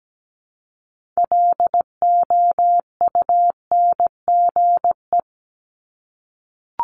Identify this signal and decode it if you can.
Morse code, heard as a keyed tone, about 17 wpm, 700 Hz. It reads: LOUNGE
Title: Multi-letter Phonemes Repeated 3x 17wpm